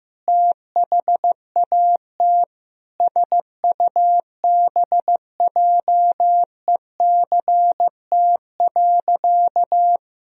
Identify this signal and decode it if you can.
Morse code, heard as a keyed tone, about 16 wpm, 700 Hz. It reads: THAT SUBJECT.
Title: Sentences from Top 1000 Words - Review Part 1 - 15wpm